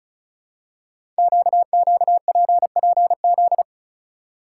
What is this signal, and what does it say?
Morse code, heard as a keyed tone, about 36 wpm, 690 Hz. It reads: QQPPZ
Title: Code Groups - 5 Letters - 35wpm